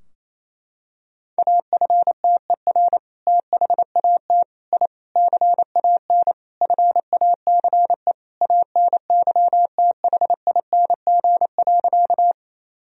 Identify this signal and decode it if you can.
Morse code, heard as a keyed tone, about 28 wpm, 700 Hz. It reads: AFTER THAT I CAN FACE ANYTHING.
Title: Sentences from Top 1000 Words - Review Part 2 - 28wpm